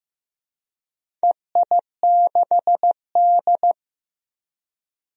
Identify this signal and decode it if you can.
Morse code, heard as a keyed tone, about 15 wpm, 700 Hz. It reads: EI6D
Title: DXCC-100 Call Signs - 4 Characters - Repeated 2x - 15wpm